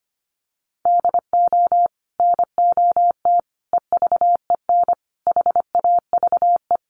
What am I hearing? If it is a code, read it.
Morse code, heard as a keyed tone, about 25 wpm, 700 Hz. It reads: DO NOT EVEN HAVE
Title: Sets of 5 Words Encore2 25wpm